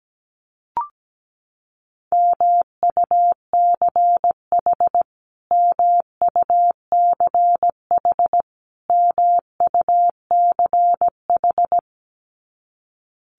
Morse code, heard as a keyed tone, about 17 wpm, 700 Hz. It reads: MUCH MUCH MUCH
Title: Top 200 Words Repeated 17wpm